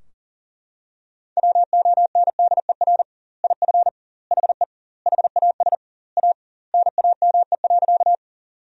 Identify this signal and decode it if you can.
Morse code, heard as a keyed tone, about 40 wpm, 700 Hz. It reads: WONDER IF HE HAS A NAME.
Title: 2x Word-Spacing - Sentences from Top 500 Words - 40wpm